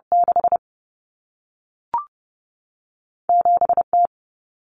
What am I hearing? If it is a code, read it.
Morse code, heard as a keyed tone, about 30 wpm, spaced as for 3 wpm, 700 Hz. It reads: 67 T
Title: Common Cut Numbers - 2-Digit 30wpm